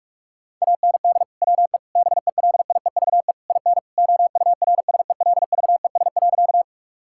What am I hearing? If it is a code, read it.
Morse code, heard as a keyed tone, about 46 wpm, 700 Hz. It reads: AND WE BELIEVE IN OURSELVES.
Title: Sentences from Top 1600 Words 45wpm